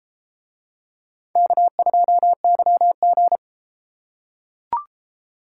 Morse code, heard as a keyed tone, about 31 wpm, 700 Hz. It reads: K2YG